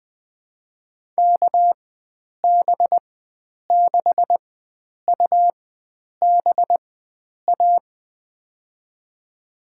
Morse code, heard as a keyed tone, about 21 wpm, spaced as for 9 wpm, 700 Hz. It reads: KB6UBA
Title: USA Call Signs - 6 Characters - 4x Character Spacing - 20wpm